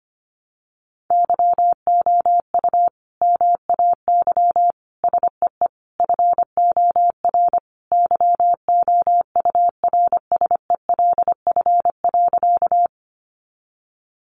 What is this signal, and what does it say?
Morse code, heard as a keyed tone, about 25 wpm, 700 Hz. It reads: YOU MAY SEE FOR YOURSELF.
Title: Sentences from Top 600 Words 25wpm